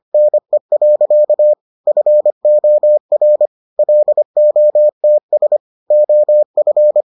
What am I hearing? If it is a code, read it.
Morse code, heard as a keyed tone, about 25 wpm, 600 Hz. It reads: NE. FOR LOTS OF